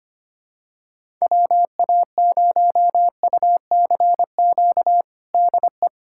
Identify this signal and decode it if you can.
Morse code, heard as a keyed tone, about 25 wpm, 700 Hz. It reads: WA0UCQ DE
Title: USA Callsign DE Callsign - 6 Characters - 25wpm